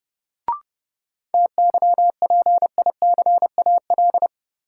Morse code, heard as a keyed tone, about 30 wpm, 700 Hz. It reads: TYPICAL
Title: Sentences from Top 1100 Words - New Word Drill Alt - 30wpm